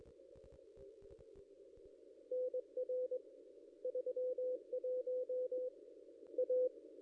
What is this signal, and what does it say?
Morse code, heard as a keyed tone, about 21 wpm, 500 Hz. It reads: NR 31 A
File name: contests / ss-cw-k5tr-op-n5rz -0100